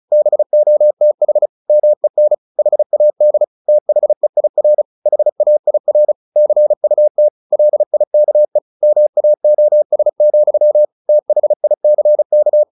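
Morse code, heard as a keyed tone, about 34 wpm, 600 Hz. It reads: BOTH MEN HAD THEIR HAIR CUT LIKE MAOS, THICK
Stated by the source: Morse practice at 35 wpm from CW med Gnister, sending Ken Follett's novel Never in English.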